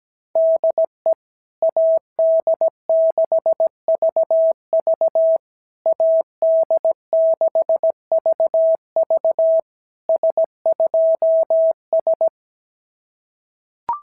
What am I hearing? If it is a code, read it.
Morse code, heard as a keyed tone, about 17 wpm, 650 Hz. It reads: DE AD6VV AD6VV S2S
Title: SOTA QSO Practice 17wpm